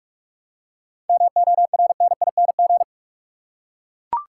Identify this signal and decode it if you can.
Morse code, heard as a keyed tone, about 42 wpm, 700 Hz. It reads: MORNING
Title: Seven Letter Words - 45wpm